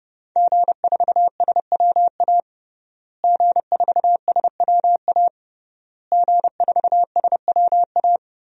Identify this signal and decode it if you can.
Morse code, heard as a keyed tone, about 32 wpm, 700 Hz. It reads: G4SWA G4SWA G4SWA
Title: DXCC-100 Call Signs - 5 Characters - Repeated 3x - 30wpm